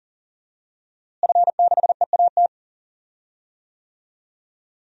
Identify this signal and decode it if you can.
Morse code, heard as a keyed tone, about 37 wpm, 700 Hz. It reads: F6EAT
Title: DXCC-100 Call Signs - 5 Characters - Repeated 3x - 40wpm